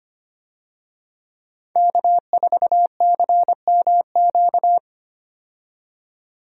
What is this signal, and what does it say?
Morse code, heard as a keyed tone, about 25 wpm, 700 Hz. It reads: K4CMQ
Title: USA Call Signs - 5 Characters - 25wpm